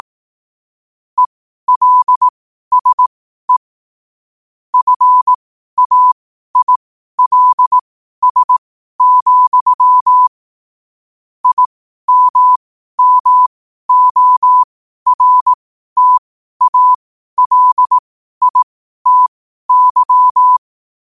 Morse code, heard as a keyed tone, about 18 wpm, spaced as for 12 wpm, 1000 Hz. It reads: ELSE FAILS, IMMORTALITY
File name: qotdfiles / 12wpm